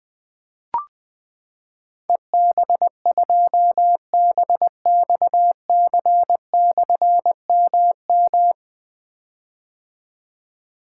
Morse code, heard as a keyed tone, about 20 wpm, 700 Hz. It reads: EB2BXC/MM
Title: Difficult Call Signs 20wpm